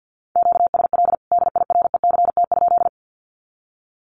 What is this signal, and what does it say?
Morse code, heard as a keyed tone, about 50 wpm, 700 Hz. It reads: QSL DIRECT?
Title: Common QSO Phrases 50wpm